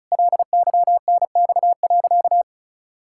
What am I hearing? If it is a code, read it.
Morse code, heard as a keyed tone, about 36 wpm, 700 Hz. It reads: LYNX.
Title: Pangram Challenge v2 - 35wpm